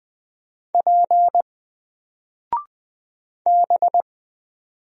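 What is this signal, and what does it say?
Morse code, heard as a keyed tone, about 20 wpm, 700 Hz. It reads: P B
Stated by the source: Single letter and number Morse drill at 20 wpm, from N4LCD.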